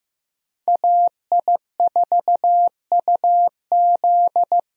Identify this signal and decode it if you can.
Morse code, heard as a keyed tone, about 16 wpm, 700 Hz. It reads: AI4UZ
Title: USA Call Signs - 5 Characters - Repeated 3x - 15wpm